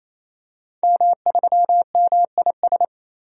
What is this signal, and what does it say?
Morse code, heard as a keyed tone, about 27 wpm, 700 Hz. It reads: M3MIS
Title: DXCC-100 Call Signs - 5 Characters - 28wpm